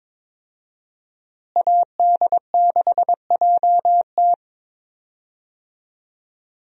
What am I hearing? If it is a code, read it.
Morse code, heard as a keyed tone, about 22 wpm, 700 Hz. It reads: AD6JT
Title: USA Call Signs - 5 Characters - Repeated 2x - 22wpm